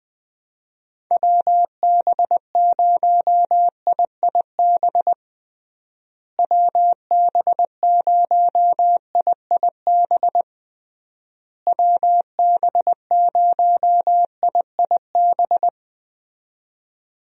Morse code, heard as a keyed tone, about 20 wpm, 700 Hz. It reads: WB0IIB WB0IIB WB0IIB
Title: USA Call Signs - 6 Characters - Repeated 3x - 20wpm